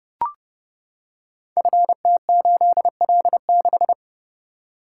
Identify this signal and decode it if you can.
Morse code, heard as a keyed tone, about 29 wpm, 700 Hz. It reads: FT8L6
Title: Code Groups - 5 Letter:Numbers - 30wpm